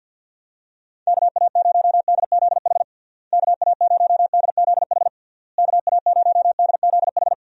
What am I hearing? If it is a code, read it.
Morse code, heard as a keyed tone, about 50 wpm, 700 Hz. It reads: KA0DZH KA0DZH KA0DZH
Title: USA Call Signs - 6 Characters - Repeated 3x - 50wpm